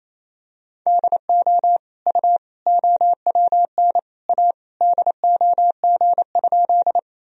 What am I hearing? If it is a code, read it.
Morse code, heard as a keyed tone, about 28 wpm, 700 Hz. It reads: DO U OWN A DOG?
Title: Sentences from Top 400 Words and Abbreviations 28wpm